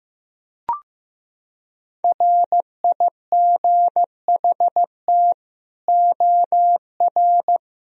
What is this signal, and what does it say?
Morse code, heard as a keyed tone, about 15 wpm, 700 Hz. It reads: RIGHT OR
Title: Sentences from Top 1000 Words - Review Part 4 - 15wpm